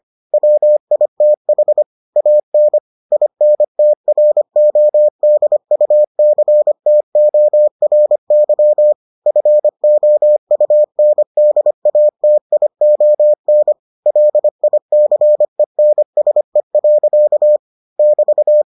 Morse code, heard as a keyed tone, about 25 wpm, 600 Hz. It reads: WITH AN INTRODUCTORY FOUNDATION LICENSE. <BT>